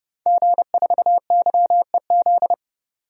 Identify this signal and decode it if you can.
Morse code, heard as a keyed tone, about 30 wpm, 700 Hz. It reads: G4YEZ
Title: DXCC-100 Call Signs - 5 Characters - Repeated 3x - 30wpm